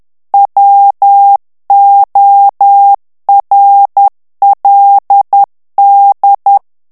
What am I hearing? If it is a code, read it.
Morse code, heard as a keyed tone, about 10 wpm, 800 Hz. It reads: WORLD
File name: ANGLAIS texte clair 3 vitesse 10 mots minute